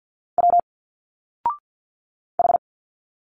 Morse code, heard as a keyed tone, about 50 wpm, 700 Hz. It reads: W H
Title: Single Letters 50wpm